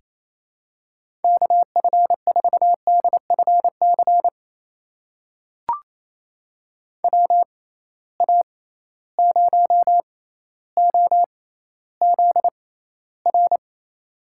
Morse code, heard as a keyed tone, about 28 wpm, spaced as for 14 wpm, 700 Hz. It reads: KF4DFC WA0OZR
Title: USA Call Signs - 6 Characters - 6x Character Spacing - 28wpm